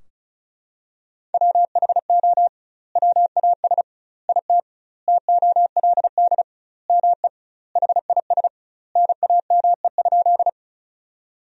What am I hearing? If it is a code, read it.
Morse code, heard as a keyed tone, about 36 wpm, 700 Hz. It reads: WHO WAS IT TOLD ME HIS NAME?
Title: 2x Word-Spacing - Sentences from Top 400 Words - 35wpm